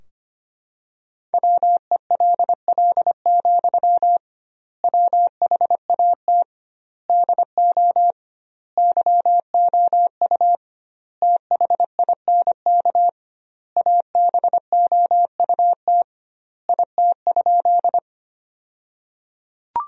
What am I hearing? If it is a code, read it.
Morse code, heard as a keyed tone, about 25 wpm, 700 Hz. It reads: WELL, WHAT DO YOU THINK ABOUT IT?
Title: 2x Word-Spacing - Sentences from Top 200 Words - 25wpm